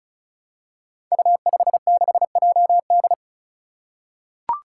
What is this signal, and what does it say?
Morse code, heard as a keyed tone, about 34 wpm, 700 Hz. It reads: U56JD
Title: Five Challenging Letter-Number Combinations 35wpm